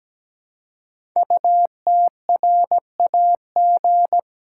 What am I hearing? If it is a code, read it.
Morse code, heard as a keyed tone, about 17 wpm, 700 Hz. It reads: UTRAG
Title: Code Groups - 5 Letter:Numbers - 17wpm